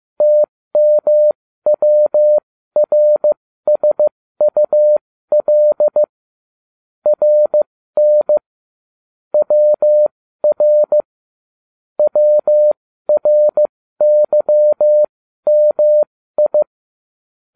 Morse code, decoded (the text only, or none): TMWRSUL RN WR WRYMI